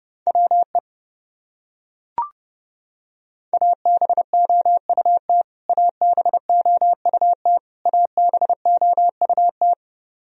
Morse code, heard as a keyed tone, about 32 wpm, 700 Hz. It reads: WE ABOUT ABOUT ABOUT
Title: Top 100 Words Repeated 30wpm